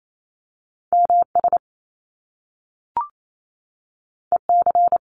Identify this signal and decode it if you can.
Morse code, heard as a keyed tone, about 28 wpm, 700 Hz. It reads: M S E C
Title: Top Two Letter Combinations in Words 28wpm